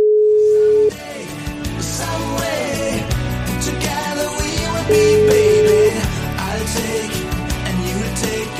Ringback tone (Europe, 425 Hz), with noise.